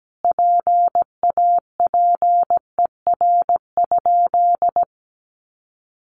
Morse code, heard as a keyed tone, about 17 wpm, 700 Hz. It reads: PAPER?
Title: Sentences from Top 1300 Words 17wpm